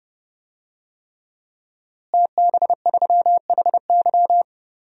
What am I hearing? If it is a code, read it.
Morse code, heard as a keyed tone, about 30 wpm, 700 Hz. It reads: TB3HY
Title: DXCC-100 Call Signs - 5 Characters - Repeated 3x - 30wpm